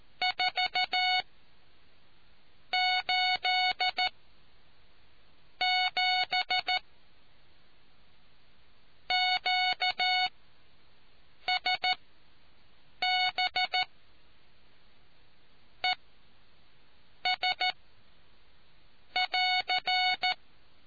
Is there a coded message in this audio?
Yes: Morse code from 0.2 to 20.3 s, heard as a keyed tone, about 14 wpm, spaced as for 5 wpm, 740 Hz. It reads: 487 QSB ES<AR>